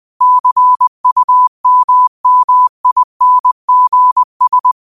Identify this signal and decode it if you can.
Morse code, heard as a keyed tone, about 20 wpm, 1000 Hz. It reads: CUMMINGS
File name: qotdfiles / 20wpm